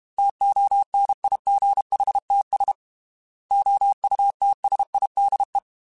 Morse code, heard as a keyed tone, about 32 wpm, 800 Hz. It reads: TONIGHTS OUTSIDE
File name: Weather Sets / Weather Sets-mp3s / Weather 4 28 @30